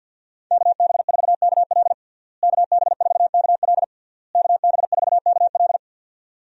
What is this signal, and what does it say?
Morse code, heard as a keyed tone, about 50 wpm, 700 Hz. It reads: KB4KL KB4KL KB4KL
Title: USA Call Signs - 5 Characters - Repeated 3x - 50wpm